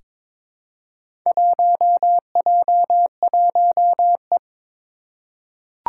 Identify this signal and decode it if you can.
Morse code, heard as a keyed tone, about 22 wpm, 700 Hz. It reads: 1J1E
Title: Four Challenging Letter-Number Combinations 22wpm